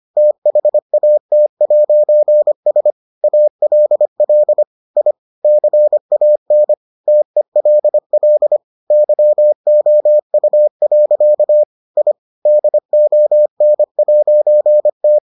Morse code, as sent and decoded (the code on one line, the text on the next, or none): - .... .- - .----. ... / .- .-.. .-.. / .. / -.-. .- -. / - . .-.. .-.. / -.-- --- ..- .-.-.- / .. / -.. --- -. .----. -
THAT'S ALL I CAN TELL YOU. I DON'T